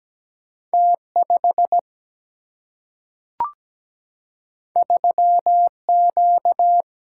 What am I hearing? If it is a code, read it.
Morse code, heard as a keyed tone, about 17 wpm, 700 Hz. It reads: T5 3Q
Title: Single-Letter Single-Number 17wpm